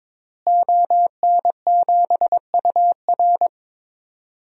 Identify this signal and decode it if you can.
Morse code, heard as a keyed tone, about 22 wpm, 700 Hz. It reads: ON7UR
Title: DXCC-100 Call Signs - 5 Characters - Repeated 3x - 22wpm